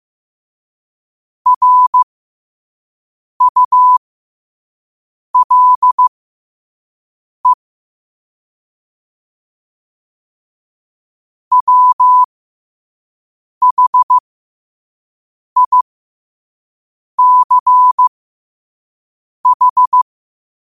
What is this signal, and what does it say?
Morse code, heard as a keyed tone, about 15 wpm, spaced as for 5 wpm, 1000 Hz. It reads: RULE WHICH
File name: 5wpm